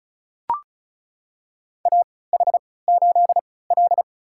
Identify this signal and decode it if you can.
Morse code, heard as a keyed tone, about 34 wpm, spaced as for 20 wpm, 700 Hz. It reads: AH8L
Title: USA Call Signs - 4 Characters - 3x Character Spacing - 35wpm